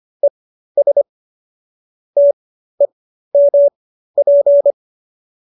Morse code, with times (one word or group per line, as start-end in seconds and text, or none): 0.2-1.0 ES
2.2-4.7 TEMP